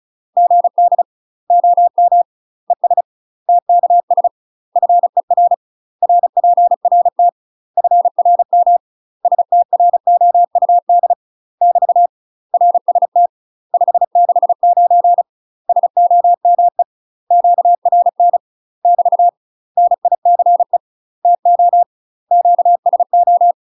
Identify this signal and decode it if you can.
Morse code, heard as a keyed tone, about 36 wpm, 700 Hz. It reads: GD OM ES TKS FER RPRT FRM STROUD <BT> RST 569 SOME QRN <BT> NICE TO QSO
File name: UK MASTER QSOs TRAINING / 35WPM / UK MASTER QSOs 0032